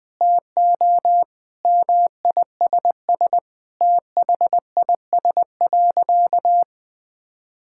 Morse code, heard as a keyed tone, about 19 wpm, 700 Hz. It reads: TO MISS THIS.
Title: Sentences with 4-Word Phrases - Spoken Beginning - 20wpm